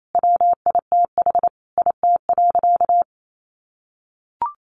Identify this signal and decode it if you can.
Morse code, heard as a keyed tone, about 28 wpm, 700 Hz. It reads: WITH IT.
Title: A Sound Of Thunder 28wpm